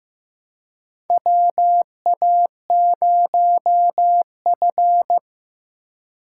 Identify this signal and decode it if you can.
Morse code, heard as a keyed tone, about 15 wpm, 700 Hz. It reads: WA0F
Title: USA Call Signs - 4 Characters - 3x Character Spacing - 15wpm